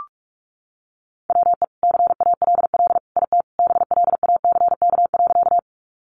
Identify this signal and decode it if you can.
Morse code, heard as a keyed tone, about 44 wpm, 700 Hz. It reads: WE CALL IT BLACK.
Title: Sentences from Top 400 Words 45wpm